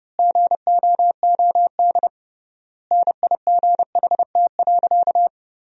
Morse code, heard as a keyed tone, about 30 wpm, 700 Hz. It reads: GOOD NIGHT.
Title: 3x Word-Spacing - Sentences from Top 300 Words - 30wpm